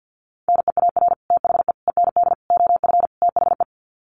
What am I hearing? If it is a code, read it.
Morse code, heard as a keyed tone, about 50 wpm, 700 Hz. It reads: NEAR THE END OF THE
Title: Sets of 5 Words Encore 50wpm